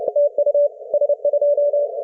Morse code, heard as a keyed tone, about 30 wpm, 580 Hz. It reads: TU S2
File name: s21zas-bangladesh-st-martin-s-island-as-127-on-15m-cw